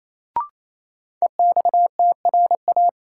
Morse code, heard as a keyed tone, about 28 wpm, 700 Hz. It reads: EXTRA
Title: Sentences from Top 1000 Words - New Word Drill - 28wpm